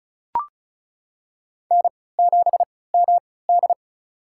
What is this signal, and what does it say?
Morse code, heard as a keyed tone, about 36 wpm, spaced as for 20 wpm, 700 Hz. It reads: N7MD
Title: USA Call Signs - 4 Characters - 3x Character Spacing - 35wpm